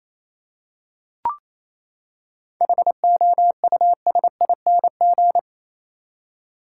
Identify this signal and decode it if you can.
Morse code, heard as a keyed tone, about 28 wpm, 700 Hz. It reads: HOUSING
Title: Seven Letter Words - 28wpm